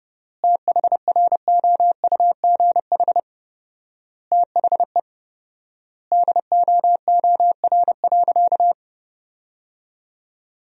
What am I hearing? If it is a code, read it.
Morse code, heard as a keyed tone, about 32 wpm, 700 Hz. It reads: THROUGH THE DOOR.
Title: 4x Word-Spacing - Sentences from Top 400 Words - 30wpm